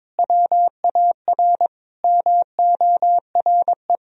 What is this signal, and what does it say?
Morse code, heard as a keyed tone, about 22 wpm, 700 Hz. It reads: WAR MORE
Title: Sentences from Top 1000 Words - Review Part 3 - 22wpm